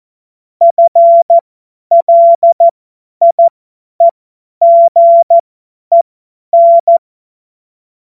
Morse code, heard as a keyed tone, about 14 wpm, spaced as for 10 wpm, 680 Hz. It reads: FLIEGEN
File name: Morsekurs / Witze / Witz 28.November 2023 50 F70